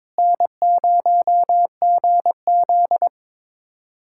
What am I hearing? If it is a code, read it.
Morse code, heard as a keyed tone, about 22 wpm, 700 Hz. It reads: N0GZ
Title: USA Callsign DE Callsign - 4 Characters - 22wpm